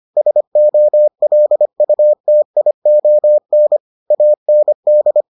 Morse code, heard as a keyed tone, about 25 wpm, 600 Hz. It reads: SOLUTION AND